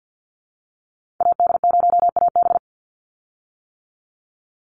Morse code, heard as a keyed tone, about 50 wpm, 700 Hz. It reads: AD0AB
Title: Call Signs - 50wpm